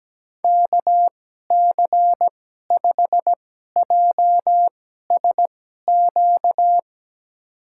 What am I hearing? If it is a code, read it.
Morse code, heard as a keyed tone, about 17 wpm, spaced as for 12 wpm, 700 Hz. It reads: KC5JSQ
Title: USA Call Signs - 6 Characters - 2x Character Spacing - 17wpm